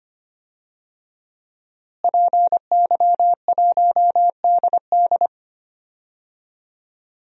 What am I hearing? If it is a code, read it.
Morse code, heard as a keyed tone, about 25 wpm, 700 Hz. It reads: PY1DD
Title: DXCC-100 Call Signs - 5 Characters - Repeated 3x - 25wpm